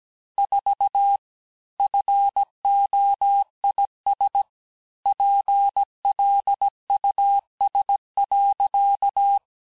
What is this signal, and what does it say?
Morse code, heard as a keyed tone, about 17 wpm, 800 Hz. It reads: 4 FOIS PLUS.
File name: corrige-cours-1581-fin-du-QSO-vitesse-16-mpm